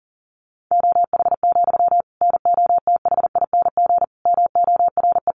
Morse code, heard as a keyed tone, about 40 wpm, 700 Hz. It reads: OH, NOTHING MORE
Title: Sentences from Top 700 Words 40wpm